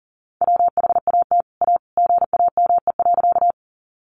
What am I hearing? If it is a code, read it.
Morse code, heard as a keyed tone, about 40 wpm, 700 Hz. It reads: WHAT A GAME.
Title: Sentences from Top 500 Words 40wpm